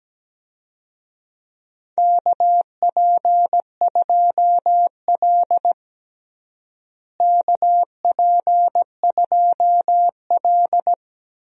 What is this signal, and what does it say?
Morse code, heard as a keyed tone, about 17 wpm, 700 Hz. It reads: KP2L KP2L
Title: DXCC-100 Call Signs - 4 Characters - Repeated 2x - 17wpm